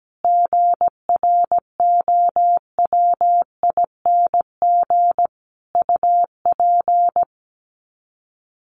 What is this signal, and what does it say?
Morse code, heard as a keyed tone, about 17 wpm, 700 Hz. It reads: GROWING UP
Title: Sets of 5 Words Encore 17wpm